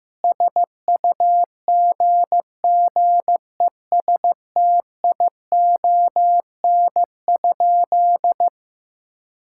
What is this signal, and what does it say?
Morse code, heard as a keyed tone, about 15 wpm, 700 Hz. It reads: SUGGESTION?
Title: Sentences from Top 1600 Words 15wpm